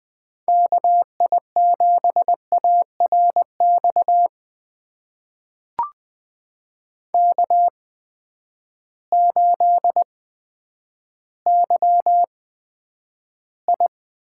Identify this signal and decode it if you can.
Morse code, heard as a keyed tone, about 20 wpm, spaced as for 10 wpm, 700 Hz. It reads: KI7ARX K8YI
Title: USA Call Signs - 8x Character Spacing - 20wpm